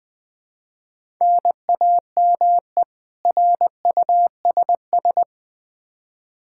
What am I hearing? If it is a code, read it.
Morse code, heard as a keyed tone, about 20 wpm, 700 Hz. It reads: NAME RUSS
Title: Practice QSO - Exchange 1 - Beginning - 20wpm